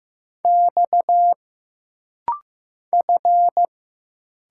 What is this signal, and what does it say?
Morse code, heard as a keyed tone, about 15 wpm, 700 Hz. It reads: X F
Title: Single Letters - Rapid-Fire 15wpm